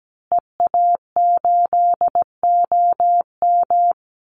Morse code, heard as a keyed tone, about 17 wpm, 700 Hz. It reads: EA8OM
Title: Call Signs Repeated 3x - 17wpm